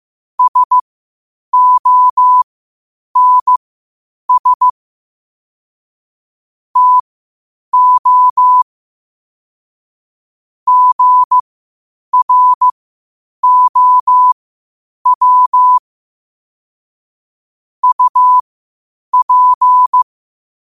Morse code, heard as a keyed tone, about 15 wpm, spaced as for 8 wpm, 1000 Hz. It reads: SONS TO GROW UP
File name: qotdfiles / 8wpm